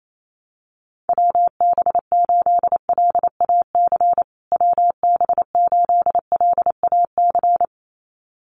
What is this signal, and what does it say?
Morse code, heard as a keyed tone, about 28 wpm, 700 Hz. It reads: WB8LAC WB8LAC
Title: USA Call Signs Repeated 2x 28wpm